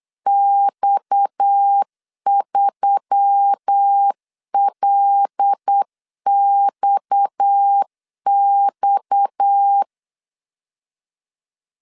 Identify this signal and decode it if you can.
Morse code, heard as a keyed tone, about 8 wpm, 800 Hz. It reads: X3LXX
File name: LEZIONI / 6 PKX0 / 6 PKX 8wpm 3041